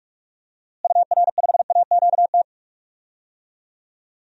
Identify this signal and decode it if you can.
Morse code, heard as a keyed tone, about 41 wpm, 700 Hz. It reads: UR5AQT
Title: DXCC-100 Call Signs - 6 Characters - Repeated 2x - 45wpm